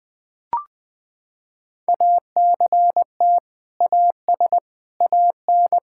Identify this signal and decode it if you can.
Morse code, heard as a keyed tone, about 21 wpm, 700 Hz. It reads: ACT AS AN
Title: Sentences with 2-Word Phrases - 20wpm